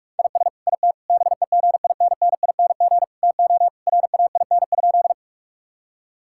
Morse code, heard as a keyed tone, about 41 wpm, 700 Hz. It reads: IS IT BEGINNING TO RAIN?
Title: Sentences from Top 1000 Words - Review Part 5 - 45wpm